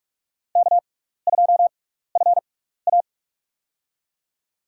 Morse code, heard as a keyed tone, about 48 wpm, spaced as for 16 wpm, 700 Hz. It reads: K2FA
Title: USA Call Signs - 4 Characters - 6x Character Spacing - 45wpm